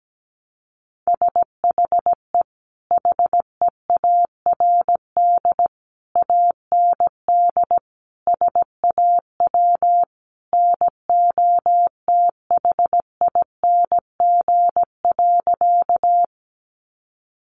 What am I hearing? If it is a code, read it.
Morse code, heard as a keyed tone, about 17 wpm, 700 Hz. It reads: SHE HEARD AND SAW NOTHING.
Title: Sentences from Top 500 Words 17wpm